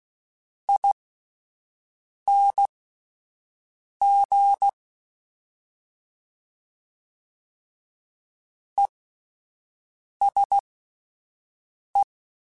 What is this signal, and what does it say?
Morse code, heard as a keyed tone, about 16 wpm, spaced as for 5 wpm, 800 Hz. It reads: ING ESE